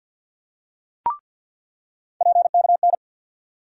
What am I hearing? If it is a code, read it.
Morse code, heard as a keyed tone, about 47 wpm, 700 Hz. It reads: PKN
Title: USA Call Sign Postfixes - 3 Letters 50wpm